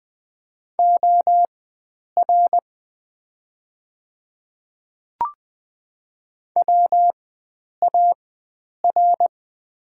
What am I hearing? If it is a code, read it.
Morse code, heard as a keyed tone, about 20 wpm, spaced as for 9 wpm, 700 Hz. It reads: OR WAR
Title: 4x Character Spacing - Top 500 Words - 20wpm